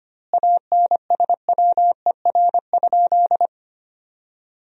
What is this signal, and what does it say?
Morse code, heard as a keyed tone, about 25 wpm, 700 Hz. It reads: ANSWER?
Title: 2x Word-Spacing - Sentences from Top 300 Words - 25wpm